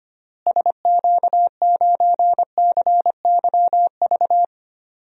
Morse code, heard as a keyed tone, about 25 wpm, 700 Hz. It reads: SQ9CYV
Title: DXCC-100 Call Signs - 6 Characters - Repeated 3x - 25wpm